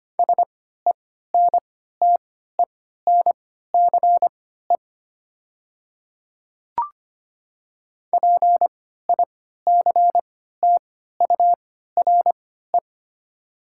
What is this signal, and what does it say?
Morse code, heard as a keyed tone, about 25 wpm, spaced as for 14 wpm, 700 Hz. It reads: SENTENCE PICTURE
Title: 3x Character Spacing - Top 200 Words - 25wpm